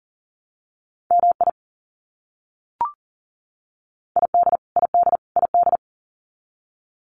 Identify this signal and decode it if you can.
Morse code, heard as a keyed tone, about 40 wpm, 700 Hz. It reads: MI ID ID ID
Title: US States Repeated 3x 40wpm